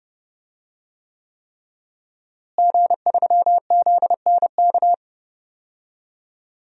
Morse code, heard as a keyed tone, about 29 wpm, 700 Hz. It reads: G3ZNK
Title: DXCC-100 Call Signs - 5 Characters - Repeated 3x - 30wpm